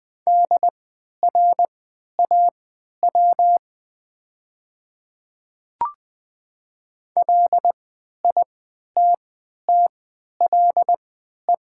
Morse code, heard as a keyed tone, about 20 wpm, spaced as for 11 wpm, 700 Hz. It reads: DRAW LITTLE
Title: 3x Character Spacing - Top 500 Words - 20wpm